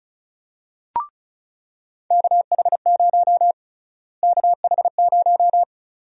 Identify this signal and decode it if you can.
Morse code, heard as a keyed tone, about 36 wpm, 700 Hz. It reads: KH0 KH0
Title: USA Call Sign Prefixes - Repeated 2x - 35wpm